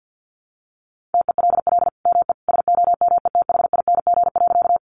Morse code, heard as a keyed tone, about 50 wpm, 700 Hz. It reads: TELL ME SOMETHING.
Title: Sentences from Top 600 Words 50wpm